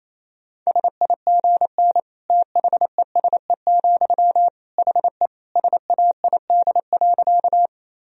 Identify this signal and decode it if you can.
Morse code, heard as a keyed tone, about 28 wpm, 700 Hz. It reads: SIGN THESE, HE SAID.
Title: Sentences from Top 1000 Words - Review Part 1 - 28wpm